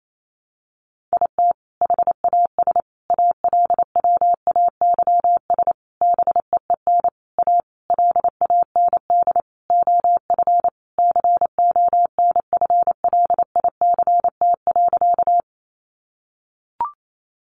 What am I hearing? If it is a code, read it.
Morse code, heard as a keyed tone, about 28 wpm, 700 Hz. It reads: IT HAS ALWAYS BEEN A LAND OF CONFLICT.
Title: Sentences from Top 1300 Words 28wpm